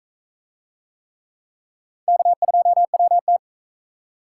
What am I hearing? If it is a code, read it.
Morse code, heard as a keyed tone, about 43 wpm, 700 Hz. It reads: K2WT